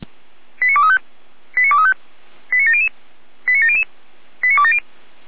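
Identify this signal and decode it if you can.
Five-tone selective call (ZVEI), heard as quick runs of five tones, tones of about 80 ms. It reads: 89235 89235 89700 89700 89270
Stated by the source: ZVEI five-tone selective call received by SDR, associated with Feuerwehr Orlamünde fire brigade.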